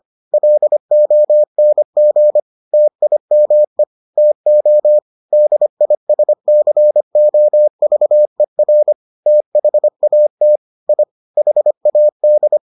Morse code, heard as a keyed tone, about 25 wpm, 600 Hz. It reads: LONG TIME TO DISCOVER THAT I HAD